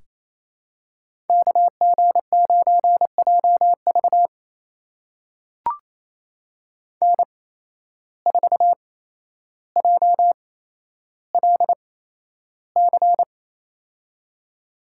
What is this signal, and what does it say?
Morse code, heard as a keyed tone, about 28 wpm, spaced as for 12 wpm, 700 Hz. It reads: KG9JV N4JLC
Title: USA Call Signs - 5 Characters - 8x Character Spacing - 28wpm